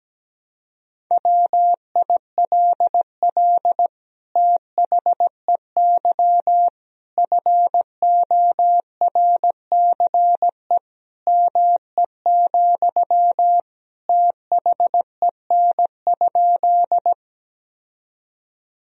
Morse code, heard as a keyed tone, about 17 wpm, 700 Hz. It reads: WILL THEY FORCE ME, THEN?
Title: Sentences from Top 1000 Words - Review Part 3 - 17wpm